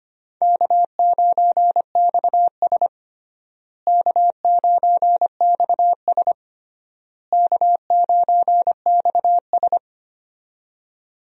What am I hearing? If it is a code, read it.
Morse code, heard as a keyed tone, about 25 wpm, 700 Hz. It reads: K9XS K9XS K9XS
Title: USA Call Signs - 4 Characters - Repeated 3x - 25wpm